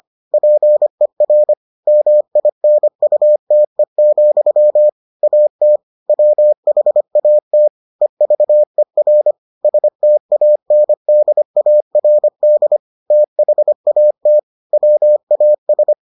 Morse code, heard as a keyed tone, about 25 wpm, 600 Hz. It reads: PER MINUTE, AT WHAT EVER STANDARD THAT WAS